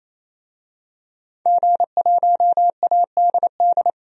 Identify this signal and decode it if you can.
Morse code, heard as a keyed tone, about 28 wpm, 700 Hz. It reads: G1ADD
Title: DXCC-100 Call Signs - 5 Characters - 28wpm